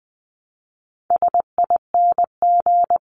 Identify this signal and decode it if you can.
Morse code, heard as a keyed tone, about 20 wpm, 700 Hz. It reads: SING
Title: Top 500 Words Repeated 20wpm